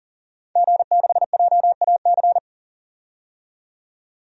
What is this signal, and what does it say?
Morse code, heard as a keyed tone, about 40 wpm, 700 Hz. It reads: G6JAC
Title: DXCC-100 Call Signs - 5 Characters - 40wpm